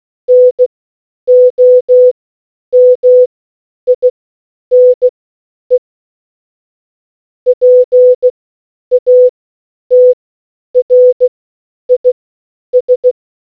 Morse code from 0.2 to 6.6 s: NOMINE